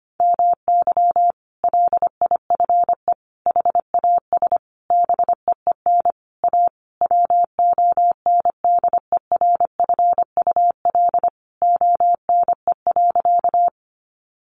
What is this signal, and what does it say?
Morse code, heard as a keyed tone, about 25 wpm, 700 Hz. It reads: MY LIFE HAS BEEN A WONDERFUL ONE.
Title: Sentences from Top 1100 Words 25wpm